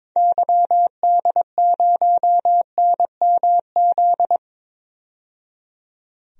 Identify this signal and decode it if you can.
Morse code, heard as a keyed tone, about 22 wpm, 700 Hz. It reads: YD0NMZ
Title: DXCC-100 Call Signs - 6 Characters - 22wpm